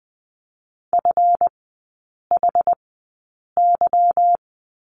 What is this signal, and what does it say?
Morse code, heard as a keyed tone, about 20 wpm, spaced as for 8 wpm, 690 Hz. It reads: FHY
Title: 198 Lesson 32e - Letter k - Three Characters 20wpm